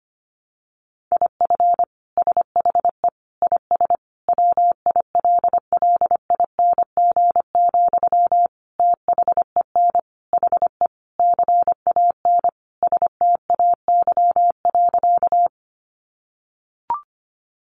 Morse code, heard as a keyed tone, about 25 wpm, 700 Hz. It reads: IF SHE IS WILLING, THEN HE CAN STAY.
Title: Sentences from Top 1000 Words 25wpm